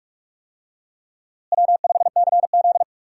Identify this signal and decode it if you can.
Morse code, heard as a keyed tone, about 41 wpm, 700 Hz. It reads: W5CZ
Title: USA Call Signs - 4 Characters - Repeated 2x - 45wpm